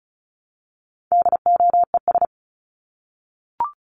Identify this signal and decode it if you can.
Morse code, heard as a keyed tone, about 34 wpm, 700 Hz. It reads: DOES
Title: ES:S Suffix - Top 100 Words with Base Words 35wpm